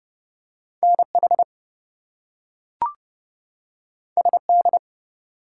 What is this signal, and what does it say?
Morse code, heard as a keyed tone, about 30 wpm, 700 Hz. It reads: NH SD
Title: US States and Canadian Provinces 30wpm